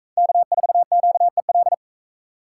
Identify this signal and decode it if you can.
Morse code, heard as a keyed tone, about 44 wpm, 700 Hz. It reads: K4QEL